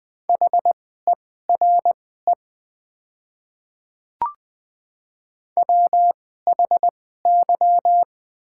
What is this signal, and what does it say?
Morse code, heard as a keyed tone, about 20 wpm, spaced as for 14 wpm, 690 Hz. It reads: HERE WHY
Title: 2x Character Spacing - Top 300 Words - 20wpm